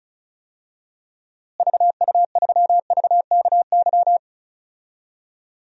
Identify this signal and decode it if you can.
Morse code, heard as a keyed tone, about 36 wpm, 700 Hz. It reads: VU3VKY
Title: DXCC-100 Call Signs - 6 Characters - Repeated 3x - 35wpm